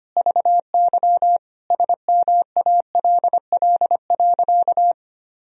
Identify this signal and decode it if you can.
Morse code, heard as a keyed tone, about 25 wpm, 700 Hz. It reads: VY SMALL.
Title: Sentences from Top 400 Words and Abbreviations 25wpm